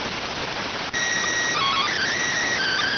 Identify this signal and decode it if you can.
SSTV (slow-scan television), PD 290.